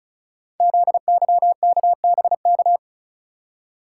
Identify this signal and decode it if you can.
Morse code, heard as a keyed tone, about 36 wpm, 690 Hz. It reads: ZYKBK
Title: Code Groups - 5 Letters - 35wpm